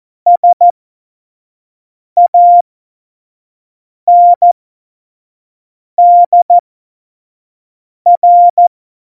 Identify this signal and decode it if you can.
Morse code, heard as a keyed tone, about 14 wpm, spaced as for 5 wpm, 700 Hz. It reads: SANDR
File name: Morsekurs / Witze / Witz 11. Juni 2019, 25 F70